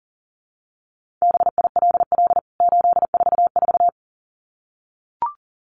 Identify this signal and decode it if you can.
Morse code, heard as a keyed tone, about 40 wpm, 700 Hz. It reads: BILL 844
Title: CWT Exchange 40wpm